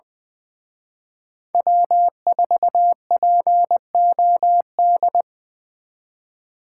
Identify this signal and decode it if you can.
Morse code, heard as a keyed tone, about 24 wpm, 700 Hz. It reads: W4POD
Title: Speed Racing - USA 1x3 Call Signs 20wpm